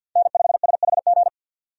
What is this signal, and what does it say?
Morse code, heard as a keyed tone, about 51 wpm, 700 Hz. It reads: N5SHG
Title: USA Call Signs - 5 Characters - 50wpm